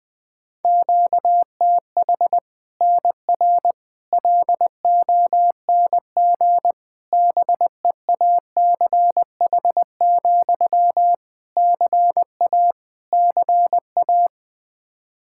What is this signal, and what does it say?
Morse code, heard as a keyed tone, about 20 wpm, 700 Hz. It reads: QTH NR LONG BEACH, CA CA
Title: K1USN SST QSO Practice 20wpm